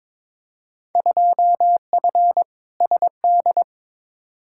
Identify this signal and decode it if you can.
Morse code, heard as a keyed tone, about 22 wpm, 700 Hz. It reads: 2F SD
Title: Field Day Exchanges Encore - 22wpm